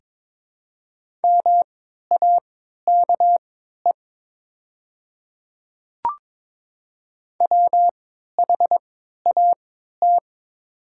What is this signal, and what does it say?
Morse code, heard as a keyed tone, about 21 wpm, spaced as for 12 wpm, 700 Hz. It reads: MAKE WHAT
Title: 3x Character Spacing - Top 100 Words - 22wpm